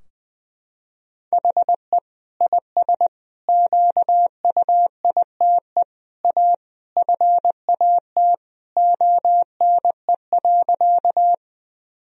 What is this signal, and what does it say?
Morse code, heard as a keyed tone, about 20 wpm, 700 Hz. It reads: HE IS QUITE A FAT ONE.
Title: Sentences from Top 1000 Words - Review Part 4 - 20wpm